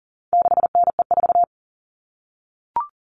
Morse code, heard as a keyed tone, about 40 wpm, 700 Hz. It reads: 6NE4
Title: All Cut Numbers - 4-Digit 40wpm